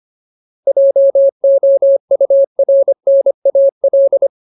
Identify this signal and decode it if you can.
Morse code, heard as a keyed tone, about 25 wpm, 550 Hz. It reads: JOURNAL
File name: Ultimate Word List / 25WPM Ultimate